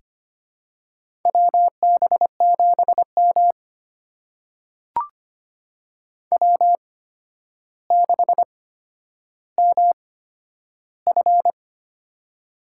Morse code, heard as a keyed tone, about 25 wpm, spaced as for 11 wpm, 700 Hz. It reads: WB7M W6MF
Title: USA Call Signs - 4 Characters - 8x Character Spacing - 25wpm